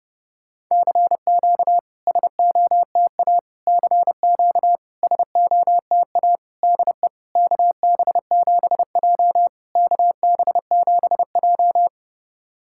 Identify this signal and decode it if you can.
Morse code, heard as a keyed tone, about 31 wpm, 690 Hz. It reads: CQ SOTA CQ SOTA DE KB7J KB7J
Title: SOTA QSO Practice 30wpm